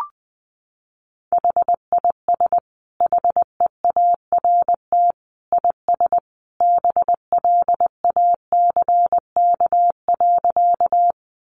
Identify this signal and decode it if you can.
Morse code, heard as a keyed tone, about 20 wpm, 700 Hz. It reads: HIS HEART IS BLACK.
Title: Sentences from Top 1200 Words 20wpm